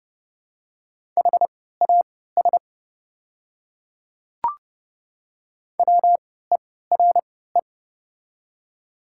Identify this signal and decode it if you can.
Morse code, heard as a keyed tone, about 30 wpm, spaced as for 17 wpm, 690 Hz. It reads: HAS WERE
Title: 3x Character Spacing - Top 100 Words - 30wpm